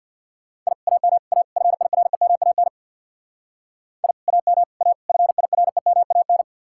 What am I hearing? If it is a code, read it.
Morse code, heard as a keyed tone, about 55 wpm, 700 Hz. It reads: I AM A FIREMAN I AM A FIREMAN
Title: Speed-Racing Common QSO Phrases - 50wpm